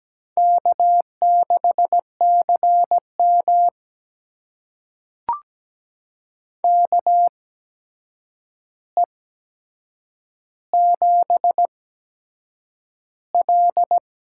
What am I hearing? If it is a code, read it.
Morse code, heard as a keyed tone, about 17 wpm, spaced as for 7 wpm, 700 Hz. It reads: K6CM KE7L
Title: USA Call Signs - 4 Characters - 8x Character Spacing - 17wpm